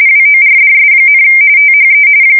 RTTY, about 45.45 baud. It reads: WORDS, THE
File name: fsk170 / morkrum